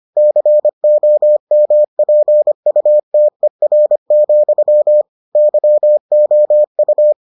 Morse code, heard as a keyed tone, about 25 wpm, 600 Hz. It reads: COMPUTER, YOU